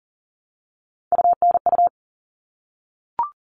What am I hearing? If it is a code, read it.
Morse code, heard as a keyed tone, about 40 wpm, 700 Hz. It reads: UNU
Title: All Cut Numbers - 3-Digit 40wpm